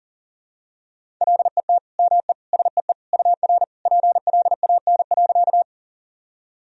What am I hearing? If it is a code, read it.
Morse code, heard as a keyed tone, about 37 wpm, 700 Hz. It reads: LET ME SEE UR PLAN.
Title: Sentences from Top 500 Words and Abbreviations 40wpm